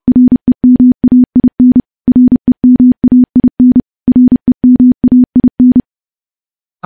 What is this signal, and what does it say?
Morse code, heard as a keyed tone, about 30 wpm, 250 Hz. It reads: REMAIN REMAIN REMAIN
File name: morse / morse-training-30WPM